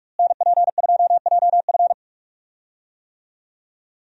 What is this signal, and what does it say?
Morse code, heard as a keyed tone, about 46 wpm, 700 Hz. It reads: NP2JF
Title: USA Call Signs - 5 Characters - Repeated 2x - 45wpm